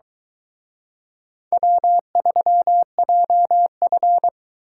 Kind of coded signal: Morse code

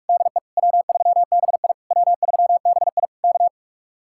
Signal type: Morse code

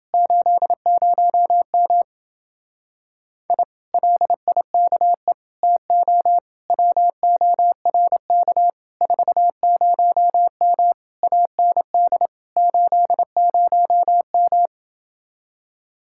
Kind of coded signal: Morse code

